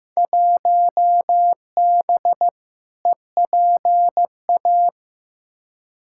Morse code, heard as a keyed tone, about 15 wpm, 700 Hz. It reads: 1B EPA
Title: Field Day Exchanges Encore - 15wpm